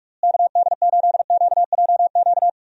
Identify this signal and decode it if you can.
Morse code, heard as a keyed tone, about 46 wpm, 700 Hz. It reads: KD8QJQ
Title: USA Call Signs - 6 Characters - Repeated 2x - 45wpm